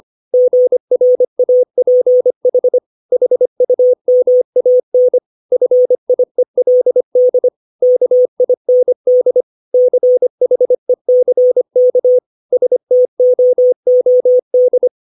Morse code, heard as a keyed tone, about 25 wpm, 500 Hz. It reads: GRAPH HUMAN FIELD KIND CHECK STOOD
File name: 632 words / 632 Words 25WPM